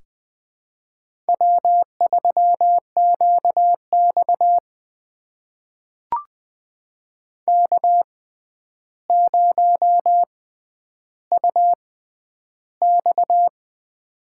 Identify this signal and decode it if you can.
Morse code, heard as a keyed tone, about 20 wpm, spaced as for 10 wpm, 700 Hz. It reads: W3QX K0UX
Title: USA Call Signs - 6x Character Spacing - 20wpm